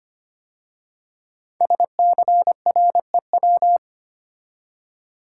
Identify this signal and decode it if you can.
Morse code, heard as a keyed tone, about 25 wpm, 700 Hz. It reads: SCREW
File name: 5 Letter Words - First 3 Letters Given - 25wpm